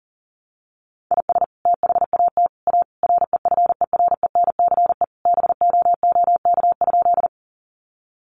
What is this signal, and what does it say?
Morse code, heard as a keyed tone, about 37 wpm, 700 Hz. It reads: IS THAT A REFERENCE BOOK?
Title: Sentences from Top 1200 Words 40wpm